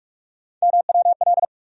Morse code, heard as a keyed tone, about 46 wpm, 700 Hz. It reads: MWL
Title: USA Call Sign Postfixes - 3 Letters 45wpm